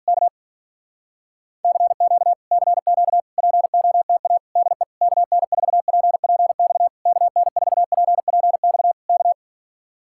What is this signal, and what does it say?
Morse code, heard as a keyed tone, about 48 wpm, 700 Hz. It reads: K CQ CQ POTA DE KN4PPX KN4PPX K